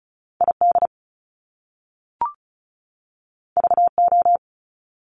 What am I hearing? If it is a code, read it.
Morse code, heard as a keyed tone, about 36 wpm, 700 Hz. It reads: ID VO
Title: Top Two Letter Combinations in Words 35wpm